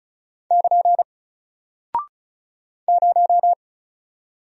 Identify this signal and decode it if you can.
Morse code, heard as a keyed tone, about 36 wpm, 700 Hz. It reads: <KN> 0
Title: Single Character (Letter:Number:Punctuation:Prosign) Alt 35wpm